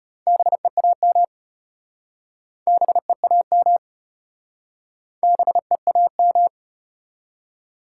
Morse code, heard as a keyed tone, about 33 wpm, 700 Hz. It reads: BEAM BEAM BEAM